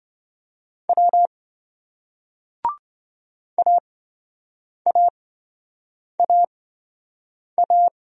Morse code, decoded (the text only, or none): W AAAA